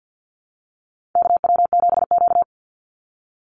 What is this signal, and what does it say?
Morse code, heard as a keyed tone, about 50 wpm, 700 Hz. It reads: KW7Q
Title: Call Signs Repeated 3x - 50wpm